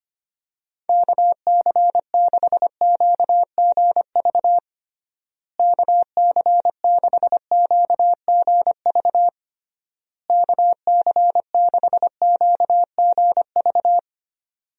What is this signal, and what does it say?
Morse code, heard as a keyed tone, about 25 wpm, 700 Hz. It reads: KC6QGV KC6QGV KC6QGV
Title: USA Call Signs - 6 Characters - Repeated 3x - 25wpm